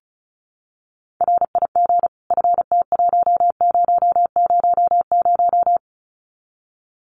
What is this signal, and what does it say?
Morse code, heard as a keyed tone, about 34 wpm, 700 Hz. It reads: RIG FT1000
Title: Common QSO Phrases 35wpm